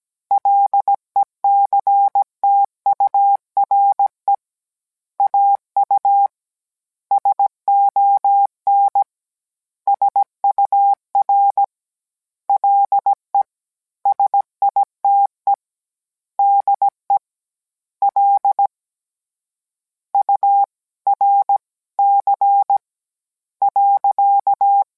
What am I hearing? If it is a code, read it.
Morse code, heard as a keyed tone, about 17 wpm, 800 Hz. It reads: LECTURE AU SON SUR LE SITE DE L U R C .